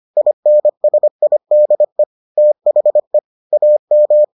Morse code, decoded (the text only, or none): INSIDE THE AM